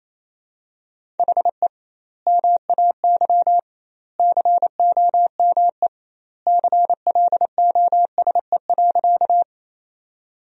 Morse code, heard as a keyed tone, about 28 wpm, 700 Hz. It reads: HE MAY COME CLOSE.
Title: 2x Word-Spacing - Sentences from Top 300 Words - 28wpm